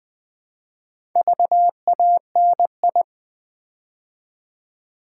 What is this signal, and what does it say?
Morse code, heard as a keyed tone, about 20 wpm, 700 Hz. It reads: VANI
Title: Build A Word - 2 Letters At A Time - 6-Letter Words - 20wpm